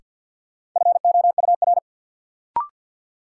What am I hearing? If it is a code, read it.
Morse code, heard as a keyed tone, about 47 wpm, 700 Hz. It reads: FOUR
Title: Sentences from Top 300 Words - New Word Drill - 50wpm